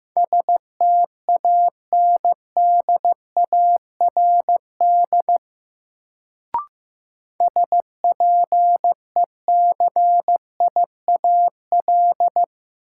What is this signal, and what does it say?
Morse code, heard as a keyed tone, about 15 wpm, 700 Hz. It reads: STANDARD SPECIAL
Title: Sentences from Top 1200 Words - New Word Drill Alt - 15wpm